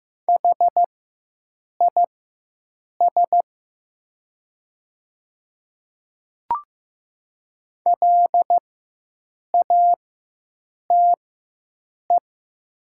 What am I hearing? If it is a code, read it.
Morse code, heard as a keyed tone, about 15 wpm, spaced as for 7 wpm, 700 Hz. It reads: HIS LATE
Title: 4x Character Spacing - Top 300 Words - 15wpm